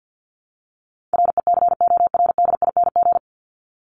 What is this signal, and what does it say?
Morse code, heard as a keyed tone, about 50 wpm, 700 Hz. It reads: RECORDING
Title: Top 1501-1750 Words 50wpm